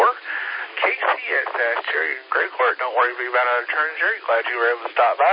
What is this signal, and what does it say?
Emergency Alert System (SAME) tones, heard as harsh buzzing bursts, with speech. It is heard from 0.2 to 4.2 s.